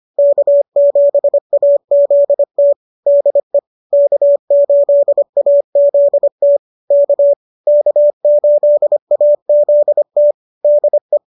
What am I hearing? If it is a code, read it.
Morse code, heard as a keyed tone, about 25 wpm, 570 Hz. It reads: K7AZT DE K8AZT K K8AZT DE